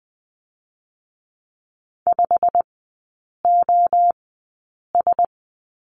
Morse code, heard as a keyed tone, about 20 wpm, spaced as for 8 wpm, 700 Hz. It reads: 5OS
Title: 100 Lesson 17e - Letter u - Three Characters 20wpm